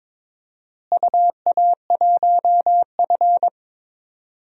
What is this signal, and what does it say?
Morse code, heard as a keyed tone, about 22 wpm, 700 Hz. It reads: UA1F
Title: DXCC-100 Call Signs - 4 Characters - Repeated 3x - 22wpm